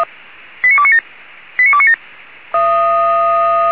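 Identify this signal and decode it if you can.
Five-tone selective call (ZVEI), heard as quick runs of five tones, tones of about 70 ms. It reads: 89297 89297
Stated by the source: ZVEI five-tone selective call 89288 received via SDR, archived on the Feuerwehr Orlamünde (f85360) audio server.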